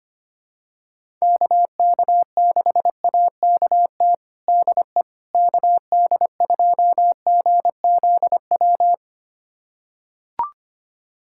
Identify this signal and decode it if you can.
Morse code, heard as a keyed tone, about 25 wpm, 700 Hz. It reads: KK6AKT DE KD2GZW
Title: USA Callsign DE Callsign - 6 Characters - 25wpm